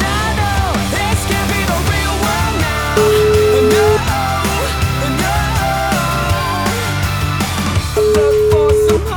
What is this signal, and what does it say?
Ringback tone (Europe, 425 Hz), with noise.